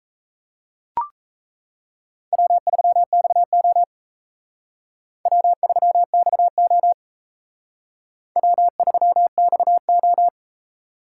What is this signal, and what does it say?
Morse code, heard as a keyed tone, about 35 wpm, 700 Hz. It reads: W3XO W3XO W3XO
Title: Speed Racing - USA 1x2 Call Signs 28wpm